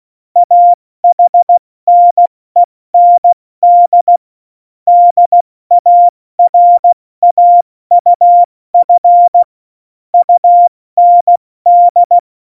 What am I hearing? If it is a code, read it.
Morse code, heard as a keyed tone, about 16 wpm, spaced as for 14 wpm, 700 Hz. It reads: AHNEND DARAUF UND
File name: Morsekurs / Witze / Witz 28 April 2020, 70 F80